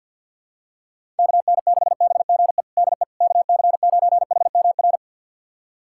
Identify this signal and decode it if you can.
Morse code, heard as a keyed tone, about 51 wpm, 700 Hz. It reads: KN6BGE DE KC9HMR
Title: USA Callsign DE Callsign 50wpm